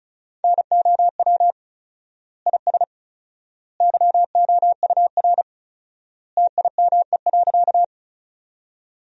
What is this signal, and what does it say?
Morse code, heard as a keyed tone, about 36 wpm, 700 Hz. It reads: NOW IS YOUR TIME.
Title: 4x Word-Spacing - Sentences from Top 100 Words - 35wpm